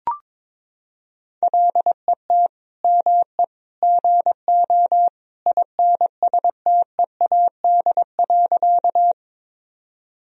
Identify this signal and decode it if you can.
Morse code, heard as a keyed tone, about 22 wpm, 700 Hz. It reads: LET ME GO INSTEAD.
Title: Sentences from Top 1000 Words - Review Part 4 - 22wpm